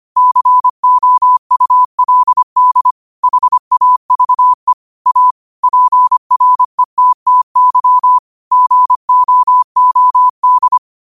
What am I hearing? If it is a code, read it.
Morse code, heard as a keyed tone, about 25 wpm, 1000 Hz. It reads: COULD HAVE A PRETTY GOOD
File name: 25wpm